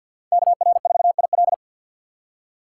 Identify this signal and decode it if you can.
Morse code, heard as a keyed tone, about 50 wpm, 700 Hz. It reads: KR4IL
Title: USA Call Signs - 5 Characters - 6x Character Spacing - 50wpm